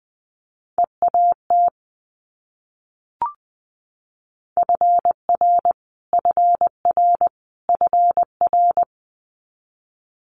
Morse code, heard as a keyed tone, about 20 wpm, 700 Hz. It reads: EAT FR FR FR
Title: Multi-letter Phonemes Repeated 3x 20wpm